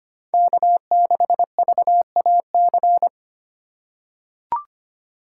Morse code, heard as a keyed tone, about 25 wpm, 700 Hz. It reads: K6VAC
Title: USA Call Signs - 2x Character Spacing - 25wpm